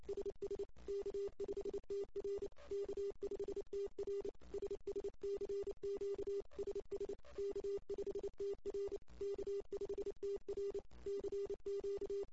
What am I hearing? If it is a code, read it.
Morse code, heard as a keyed tone, about 29 wpm, 400 Hz. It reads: SS K5TR K5TR SS CQ SS K5TR K5TR CQ
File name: ss-cw